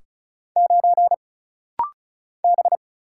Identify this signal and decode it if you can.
Morse code, heard as a keyed tone, about 36 wpm, 700 Hz. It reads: 9 B
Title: Single Letter-Number - Rapid-Fire 35wpm